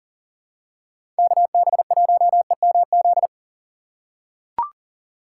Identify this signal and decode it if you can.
Morse code, heard as a keyed tone, about 40 wpm, 700 Hz. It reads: KB1EMZ
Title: USA Call Signs - 6 Characters - 3x Character Spacing - 40wpm